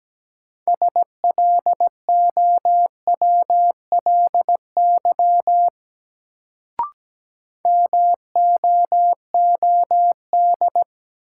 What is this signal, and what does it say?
Morse code, heard as a keyed tone, about 17 wpm, 700 Hz. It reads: SLOWLY MOOD
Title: Sentences from Top 1100 Words - New Word Drill - 17wpm